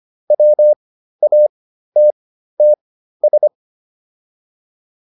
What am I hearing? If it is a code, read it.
Morse code, heard as a keyed tone, about 25 wpm, spaced as for 13 wpm, 600 Hz. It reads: WATTS
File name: QSO402 13